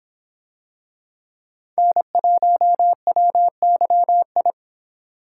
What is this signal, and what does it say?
Morse code, heard as a keyed tone, about 26 wpm, 700 Hz. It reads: N1WYI